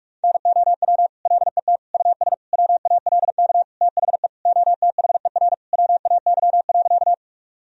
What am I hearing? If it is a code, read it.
Morse code, heard as a keyed tone, about 46 wpm, 700 Hz. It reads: NOW LET US WALK THE OTHER WAY.
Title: Sentences from Top 1000 Words - Review Part 1 - 45wpm